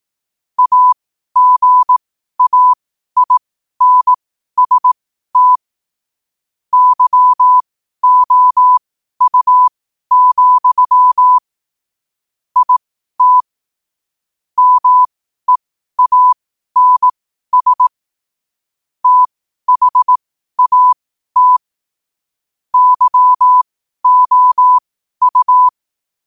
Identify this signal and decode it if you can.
Morse code, heard as a keyed tone, about 18 wpm, spaced as for 12 wpm, 1000 Hz. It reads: AGAINST YOU, IT MEANS THAT YOU
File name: qotdfiles / 12wpm